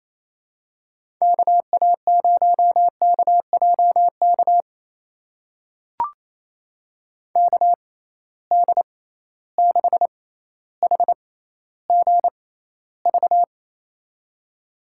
Morse code, heard as a keyed tone, about 28 wpm, spaced as for 14 wpm, 700 Hz. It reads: KA0KJK KD6HGV
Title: USA Call Signs - 6 Characters - 6x Character Spacing - 28wpm